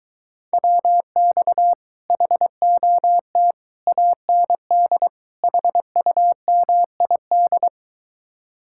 Morse code, heard as a keyed tone, about 23 wpm, 700 Hz. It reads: WX HOT AND HUMID